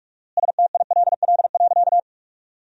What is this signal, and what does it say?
Morse code, heard as a keyed tone, about 47 wpm, 690 Hz. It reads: STILL.
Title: 4x Word-Spacing - Sentences from Top 500 Words - 45wpm